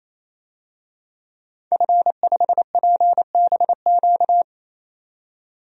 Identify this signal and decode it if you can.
Morse code, heard as a keyed tone, about 28 wpm, 700 Hz. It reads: F5PBQ
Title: DXCC-100 Call Signs - 5 Characters - 28wpm